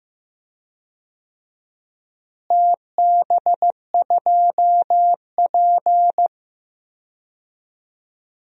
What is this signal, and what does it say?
Morse code, heard as a keyed tone, about 15 wpm, 700 Hz. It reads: TB2P
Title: DXCC-100 Call Signs - 4 Characters - Repeated 2x - 15wpm